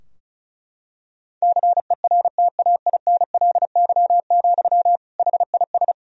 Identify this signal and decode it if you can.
Morse code, heard as a keyed tone, about 36 wpm, 700 Hz. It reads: CERTAINLY, HIS
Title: Sentences from Top 1000 Words - Review Part 2 - 35wpm